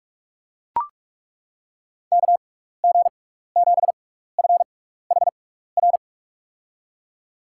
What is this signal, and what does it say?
Morse code, heard as a keyed tone, about 42 wpm, spaced as for 15 wpm, 690 Hz. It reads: KG7FHR
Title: USA Call Signs - 6x Character Spacing - 45wpm